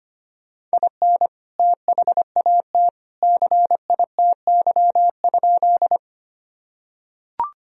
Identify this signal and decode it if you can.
Morse code, heard as a keyed tone, about 25 wpm, 700 Hz. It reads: IN THAT CITY?
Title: Sentences from Top 1000 Words - Review Part 1 - 25wpm